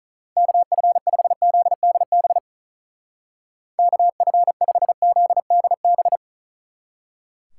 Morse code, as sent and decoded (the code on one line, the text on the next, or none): -.- ..-. ..... --.. -.. -... / -.- ..-. ..... --.. -.. -...
KF5ZDB KF5ZDB